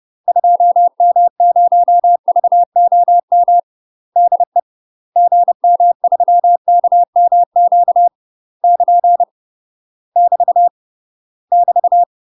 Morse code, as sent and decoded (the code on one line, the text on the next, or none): ..--- -- ----- ...- --- -- / -.. / --. -- ...-- -.- -- --.- / -.--. / -...- / -...-
2M0VOM DE GM3KMQ <KN> <BT> <BT>